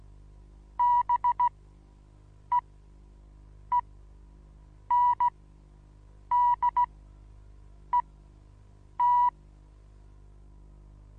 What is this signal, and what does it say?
Morse code, heard as a keyed tone, about 16 wpm, spaced as for 7 wpm, 970 Hz. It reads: BEENDET
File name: Onlinelehrgang / morsen / DJ4UF Morsen25